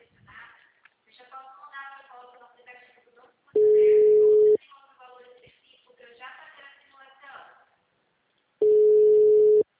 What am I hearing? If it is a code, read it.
Ringback tone (Europe, 425 Hz), with speech.
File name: out- -2010- - - .209